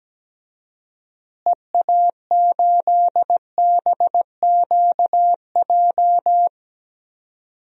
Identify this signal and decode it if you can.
Morse code, heard as a keyed tone, about 17 wpm, 700 Hz. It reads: EA8BQJ
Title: DXCC-100 Call Signs - 6 Characters - Repeated 2x - 17wpm